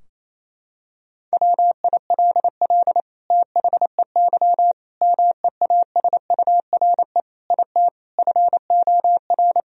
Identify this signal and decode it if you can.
Morse code, heard as a keyed tone, about 28 wpm, 700 Hz. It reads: WILL THEY MEASURE IT FOR
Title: Sentences from Top 1000 Words - Review Part 2 - 28wpm